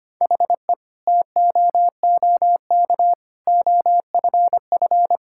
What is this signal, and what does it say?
Morse code, heard as a keyed tone, about 25 wpm, 700 Hz. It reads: HE TOOK OFF
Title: Sentences from Top 1600 Words 25wpm